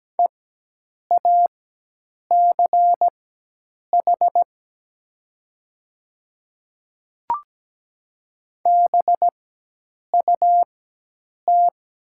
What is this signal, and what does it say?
Morse code, heard as a keyed tone, about 17 wpm, spaced as for 8 wpm, 700 Hz. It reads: EACH BUT
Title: 4x Character Spacing - Top 200 Words - 17wpm